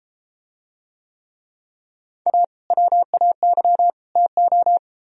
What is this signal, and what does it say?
Morse code, heard as a keyed tone, about 34 wpm, 700 Hz. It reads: A WAY TO